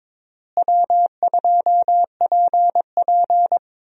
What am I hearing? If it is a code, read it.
Morse code, heard as a keyed tone, about 22 wpm, 700 Hz. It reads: W2PP
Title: USA Call Signs - 4 Characters - Repeated 3x - 22wpm